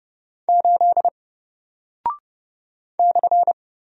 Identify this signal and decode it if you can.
Morse code, heard as a keyed tone, about 30 wpm, 700 Hz. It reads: 8 /
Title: Single Character (Letter:Number:Punctuation:Prosign) 30wpm